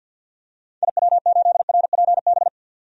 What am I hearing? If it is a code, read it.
Morse code, heard as a keyed tone, about 52 wpm, 700 Hz. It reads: IW8RPB
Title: DXCC-100 Call Signs - 6 Characters - 50wpm